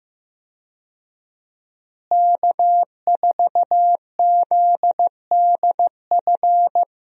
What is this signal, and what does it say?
Morse code, heard as a keyed tone, about 15 wpm, 700 Hz. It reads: K4ZDF
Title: USA Call Signs - 5 Characters - Repeated 3x - 15wpm